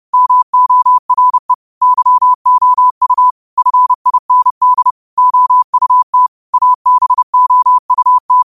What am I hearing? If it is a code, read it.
Morse code, heard as a keyed tone, about 30 wpm, 1000 Hz. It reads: MORE YOU FIND OUT ABOUT
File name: qotdfiles / 30wpm